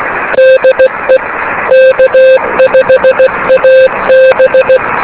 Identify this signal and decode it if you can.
Morse code, heard as a keyed tone, about 16 wpm, 520 Hz, with noise. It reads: DE K5AB
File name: beacons / k5ab